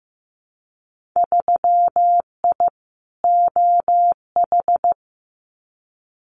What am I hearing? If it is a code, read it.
Morse code, heard as a keyed tone, about 15 wpm, 700 Hz. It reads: 3I OH
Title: Winter Field Day Exchanges Encore 15wpm